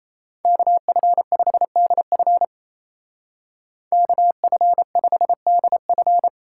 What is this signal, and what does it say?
Morse code, heard as a keyed tone, about 33 wpm, 700 Hz. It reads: KF5DF KF5DF
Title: Speed Racing - USA 2x2 Call Signs 28wpm